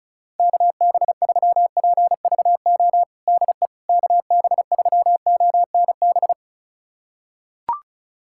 Morse code, heard as a keyed tone, about 34 wpm, 690 Hz. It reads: KB3PVO DE KB3ONB
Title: USA Callsign DE Callsign - 6 Characters - 35wpm